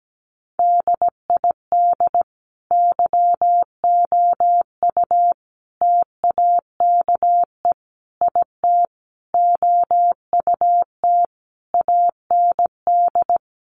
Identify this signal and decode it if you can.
Morse code, heard as a keyed tone, about 17 wpm, 700 Hz. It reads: DID YOU TAKE IT OUT AND
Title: Sentences from Top 1100 Words 17wpm